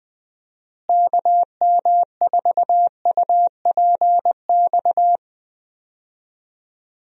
Morse code, heard as a keyed tone, about 20 wpm, 700 Hz. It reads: KM4UPX
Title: USA Call Signs - 6 Characters - Repeated 2x - 20wpm